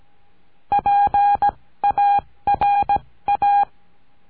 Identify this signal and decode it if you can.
Morse code, heard as a keyed tone, about 17 wpm, spaced as for 14 wpm, 810 Hz. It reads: PARA
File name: CURSO DE TELEGRAFIA